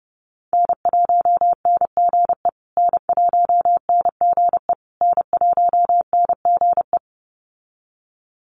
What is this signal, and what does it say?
Morse code, heard as a keyed tone, about 30 wpm, 700 Hz. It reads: N1NGE N1NGE N1NGE
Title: USA Call Signs Repeated 3x 30wpm